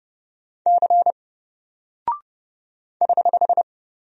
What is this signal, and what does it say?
Morse code, heard as a keyed tone, about 30 wpm, 700 Hz. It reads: C <HH>
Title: Single Character (Letter:Number:Punctuation:Prosign) 30wpm